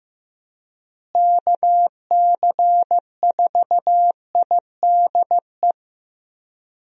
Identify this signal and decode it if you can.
Morse code, heard as a keyed tone, about 15 wpm, 700 Hz. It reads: KC4IDE
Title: USA Call Signs - 6 Characters - Repeated 2x - 15wpm